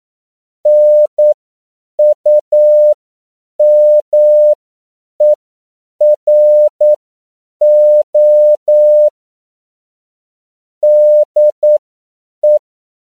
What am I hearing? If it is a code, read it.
Morse code, heard as a keyed tone, about 9 wpm, spaced as for 7 wpm, 600 Hz. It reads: NUMERO DE
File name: CW / 10PPM